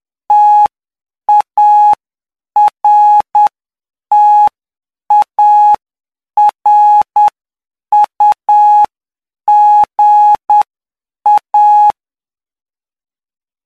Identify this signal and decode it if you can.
Morse code, heard as a keyed tone, about 9 wpm, spaced as for 7 wpm, 840 Hz. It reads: TARTARUGA